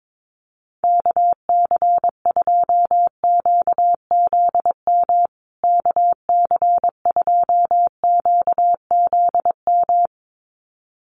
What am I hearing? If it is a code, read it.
Morse code, heard as a keyed tone, about 22 wpm, 700 Hz. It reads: KC2QZM KC2QZM
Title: USA Call Signs Repeated 2x 22wpm